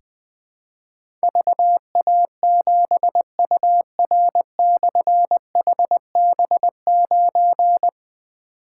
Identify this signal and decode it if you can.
Morse code, heard as a keyed tone, about 20 wpm, 700 Hz. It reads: VA7UR/HB9
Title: Difficult Call Signs 20wpm